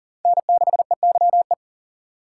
Morse code, heard as a keyed tone, about 40 wpm, 700 Hz. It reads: N6EYE
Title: USA Call Signs - 5 Characters - 6x Character Spacing - 40wpm